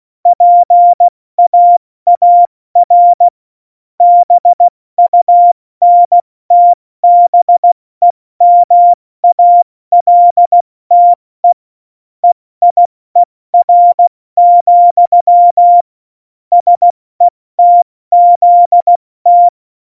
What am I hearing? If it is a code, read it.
Morse code, heard as a keyed tone, about 16 wpm, 700 Hz. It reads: PAAR BUNTBEMALTE EIER, SETZT
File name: Morsekurs / Witze / Witz 28 April 2020, 70 F80